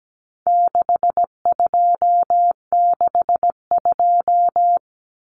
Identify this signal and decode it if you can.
Morse code, heard as a keyed tone, about 17 wpm, 700 Hz. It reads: 6262
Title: Numbers - 4-Digit 17wpm